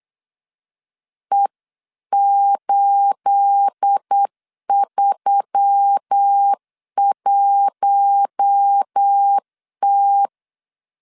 Morse code, heard as a keyed tone, about 8 wpm, 800 Hz. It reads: E831T
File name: CW / DET / 1 9 1500C S B 8wpm 5624